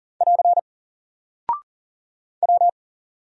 Morse code, heard as a keyed tone, about 40 wpm, 700 Hz. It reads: <AR> W
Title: Single Character (Letter:Number:Punctuation:Prosign) Alt 40wpm